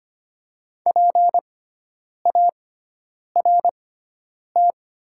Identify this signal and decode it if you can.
Morse code, heard as a keyed tone, about 25 wpm, spaced as for 9 wpm, 700 Hz. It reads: PART
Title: 6x Character Spacing - Top 200 Words - 25wpm